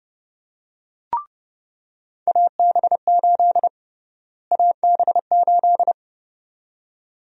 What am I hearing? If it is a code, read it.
Morse code, heard as a keyed tone, about 30 wpm, 700 Hz. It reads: AB8 AB8
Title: USA Call Sign Prefixes - Repeated 2x - 30wpm